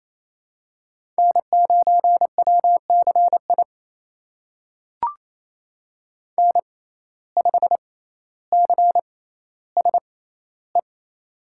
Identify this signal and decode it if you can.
Morse code, heard as a keyed tone, about 28 wpm, spaced as for 14 wpm, 700 Hz. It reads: N9WCI N5CSE
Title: USA Call Signs - 5 Characters - 6x Character Spacing - 28wpm